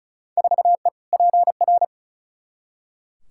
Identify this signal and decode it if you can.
Morse code, heard as a keyed tone, about 36 wpm, 700 Hz. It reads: 4E PR
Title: Field Day Exchanges Encore - 35wpm